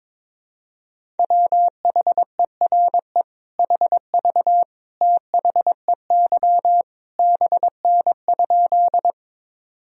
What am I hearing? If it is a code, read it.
Morse code, heard as a keyed tone, about 22 wpm, 700 Hz. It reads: WHERE HV THEY BN?
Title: Sentences from Top 200 Words and Abbreviations 22wpm